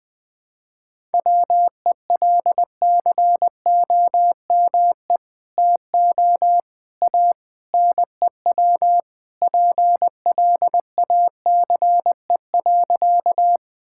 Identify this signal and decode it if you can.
Morse code, heard as a keyed tone, about 20 wpm, 700 Hz. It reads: WELCOME TO A NEW PLACE.
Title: Sentences with 2-Word Phrases - 20wpm